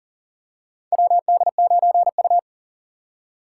Morse code, heard as a keyed tone, about 40 wpm, 700 Hz. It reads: WD9U
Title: USA Call Signs - 2x Character Spacing - 40wpm